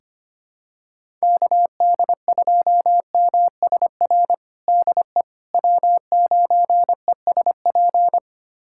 Morse code, heard as a keyed tone, about 25 wpm, 700 Hz. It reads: KD2MSR DE W9ESP
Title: USA Callsign DE Callsign 25wpm